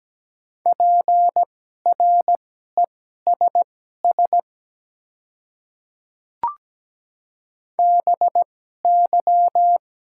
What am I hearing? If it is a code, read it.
Morse code, heard as a keyed tone, about 17 wpm, spaced as for 12 wpm, 690 Hz. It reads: PRESS BY
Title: 2x Character Spacing - Top 300 Words - 17wpm